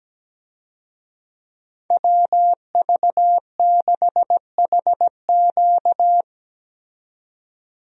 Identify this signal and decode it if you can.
Morse code, heard as a keyed tone, about 17 wpm, 700 Hz. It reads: WV6HQ
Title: USA Call Signs - 5 Characters - 17wpm